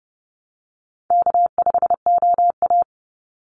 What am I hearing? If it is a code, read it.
Morse code, heard as a keyed tone, about 30 wpm, 700 Hz. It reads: K5OA
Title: Easy Call Signs - 30wpm